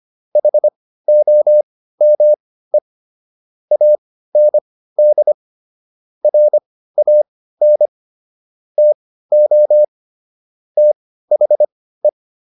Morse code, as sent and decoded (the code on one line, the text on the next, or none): .... --- -- . / .- -. -.. / .-. .- -. / - --- / - .... .
HOME AND RAN TO THE